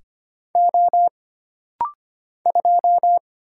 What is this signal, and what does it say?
Morse code, heard as a keyed tone, about 25 wpm, 700 Hz. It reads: O 2
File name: Single Character (Letter:Number:Punctuation:Prosign) - Rapid-Fire - 25wpm